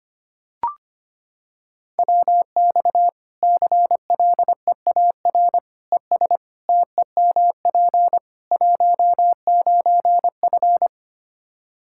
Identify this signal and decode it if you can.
Morse code, heard as a keyed tone, about 25 wpm, 700 Hz. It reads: WX CLEAR ES TEMP 19F
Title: Practice QSO - Exchange 2 - Info - 25wpm